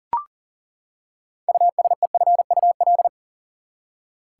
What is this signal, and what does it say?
Morse code, heard as a keyed tone, about 40 wpm, 700 Hz. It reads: USEFUL
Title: Six Letter Words - 40wpm